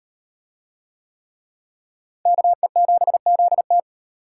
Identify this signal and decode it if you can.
Morse code, heard as a keyed tone, about 37 wpm, 700 Hz. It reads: KE7ZT